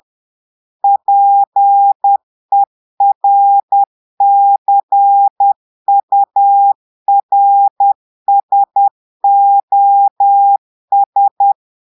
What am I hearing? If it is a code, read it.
Morse code, heard as a keyed tone, about 10 wpm, 800 Hz. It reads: PERCURSOS
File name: Portugais - texte en clair 1 - vitesse - 10 mots minute